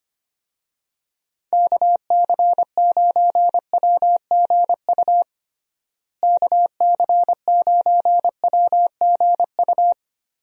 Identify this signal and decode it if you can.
Morse code, heard as a keyed tone, about 25 wpm, 700 Hz. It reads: KC9WGU KC9WGU
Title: USA Call Signs - 6 Characters - Repeated 3x - 25wpm